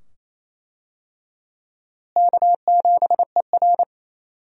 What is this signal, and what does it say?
Morse code, heard as a keyed tone, about 28 wpm, 700 Hz. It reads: K7ER
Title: USA Call Signs - 4 Characters - Repeated 2x - 28wpm